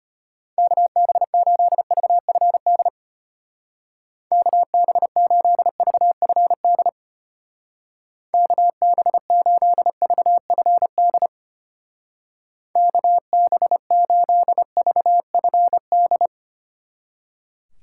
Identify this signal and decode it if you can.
Morse code, heard as a keyed tone, about 33 wpm, 700 Hz. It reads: KB8VFD KB8VFD KB8VFD KB8VFD